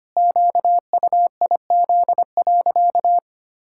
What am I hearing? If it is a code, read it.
Morse code, heard as a keyed tone, about 25 wpm, 700 Hz. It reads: QUIZ.
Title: Pangram Challenge v2 - 25wpm